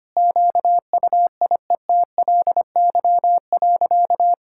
Morse code, heard as a keyed tone, about 25 wpm, 690 Hz. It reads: QUIETLY.
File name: practice / 25wpm / Sentences with 2-Word Phrases - 25wpm